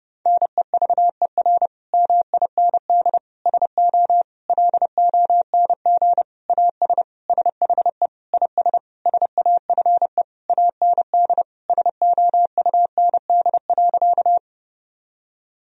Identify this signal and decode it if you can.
Morse code, heard as a keyed tone, about 30 wpm, 700 Hz. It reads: NEVER MIND SO LONG AS SHE IS SAFE AND SOUND.
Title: Sentences from Top 1000 Words - Review Part 1 - 30wpm